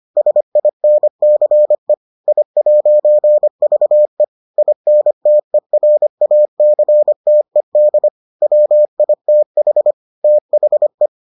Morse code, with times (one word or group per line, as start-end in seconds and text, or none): 0.2-1.9 SINCE
2.3-4.2 I'VE
4.6-8.1 INTERACTED
8.4-9.9 WITH
10.2-11.1 THE